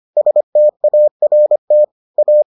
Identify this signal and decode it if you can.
Morse code, heard as a keyed tone, about 25 wpm, 600 Hz. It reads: START A